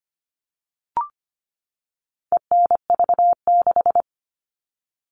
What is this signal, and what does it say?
Morse code, heard as a keyed tone, about 25 wpm, 700 Hz. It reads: ENV6
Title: All Cut Numbers - 4-Digit 25wpm